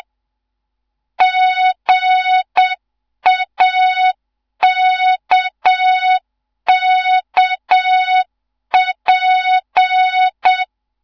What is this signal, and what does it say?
Morse code, heard as a keyed tone, about 7 wpm, 3700 Hz, with noise. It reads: GAKKP
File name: Morze-HA2MN / hang / 15 GYAKORLAT